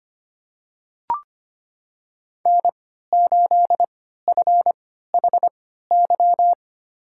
Morse code, heard as a keyed tone, about 25 wpm, spaced as for 14 wpm, 700 Hz. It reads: N8FHY
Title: USA Call Signs - 5 Characters - 3x Character Spacing - 25wpm